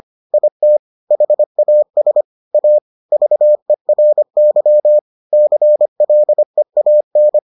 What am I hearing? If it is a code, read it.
Morse code, heard as a keyed tone, about 25 wpm, 600 Hz. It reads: IT HAS A VERY CLEAN